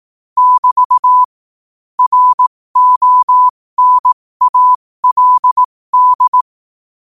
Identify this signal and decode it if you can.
Morse code, heard as a keyed tone, about 18 wpm, spaced as for 15 wpm, 1000 Hz. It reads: <BT> RONALD
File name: qotdfiles / 15wpm